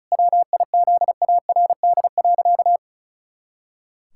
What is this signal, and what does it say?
Morse code, heard as a keyed tone, about 36 wpm, 700 Hz. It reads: WIZARD.
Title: Pangram Challenge v2 - 35wpm